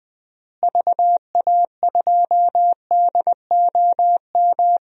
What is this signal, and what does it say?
Morse code, heard as a keyed tone, about 20 wpm, 700 Hz. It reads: VA2DOM
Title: DXCC-100 Call Signs - 6 Characters - Repeated 3x - 20wpm